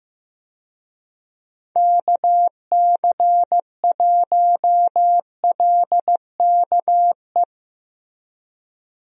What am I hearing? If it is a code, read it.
Morse code, heard as a keyed tone, about 15 wpm, 700 Hz. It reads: KC1LKE
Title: USA Call Signs - 6 Characters - Repeated 3x - 15wpm